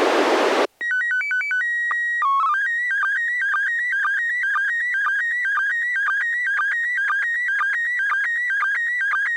SSTV (slow-scan television), PD 120.